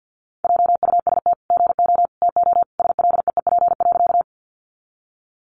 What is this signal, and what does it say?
Morse code, heard as a keyed tone, about 50 wpm, 700 Hz. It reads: JUST GO TO SLEEP.
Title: Sentences from Top 500 Words 50wpm